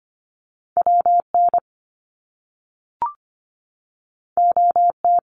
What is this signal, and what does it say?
Morse code, heard as a keyed tone, about 25 wpm, 700 Hz. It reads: WN OT
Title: Top Two Letter Combinations in Words 25wpm